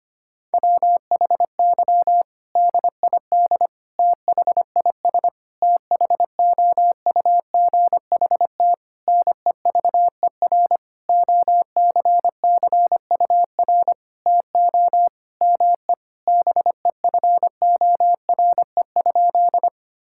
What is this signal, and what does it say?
Morse code, heard as a keyed tone, about 25 wpm, 700 Hz. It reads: WHY DID THIS THOUGHT NEVER OCCUR TO ME BEFORE?
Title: Sentences from Top 1000 Words - Review Part 6 - 25wpm